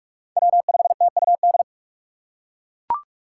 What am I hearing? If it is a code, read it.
Morse code, heard as a keyed tone, about 40 wpm, 700 Hz. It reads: W5TUD